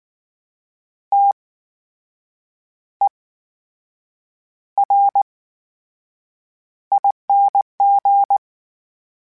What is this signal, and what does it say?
Morse code, heard as a keyed tone, about 19 wpm, spaced as for 7 wpm, 800 Hz. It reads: TERING
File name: Special INGs / Special INGs-mp3s / VerySlowToFaster Ing @18